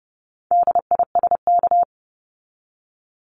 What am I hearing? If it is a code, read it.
Morse code, heard as a keyed tone, about 30 wpm, 700 Hz. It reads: DISK
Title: Top 751-1000 Words 30wpm